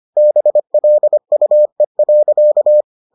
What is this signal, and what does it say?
Morse code, heard as a keyed tone, about 25 wpm, 600 Hz. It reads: BLUE.